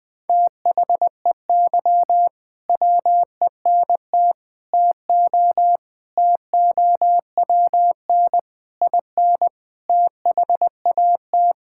Morse code, heard as a keyed tone, about 20 wpm, 700 Hz. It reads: THEY WENT TO TOWN IN THAT
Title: Sentences from Top 1000 Words - Review Part 5 - 20wpm